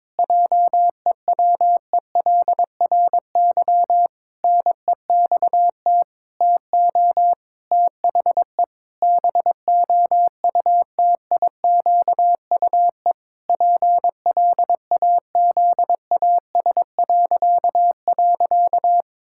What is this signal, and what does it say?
Morse code, heard as a keyed tone, about 22 wpm, 700 Hz. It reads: JEWELRY NEXT TO THE BOUTIQUE PLAZAS..
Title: Pangram Challenge v2 - 22wpm